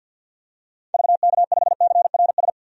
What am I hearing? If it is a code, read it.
Morse code, heard as a keyed tone, about 50 wpm, 700 Hz. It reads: VK5CRS
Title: DXCC-100 Call Signs - 6 Characters - 50wpm